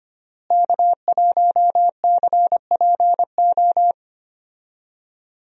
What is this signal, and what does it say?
Morse code, heard as a keyed tone, about 25 wpm, 700 Hz. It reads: K1CPO
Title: USA Call Signs - 5 Characters - Repeated 2x - 25wpm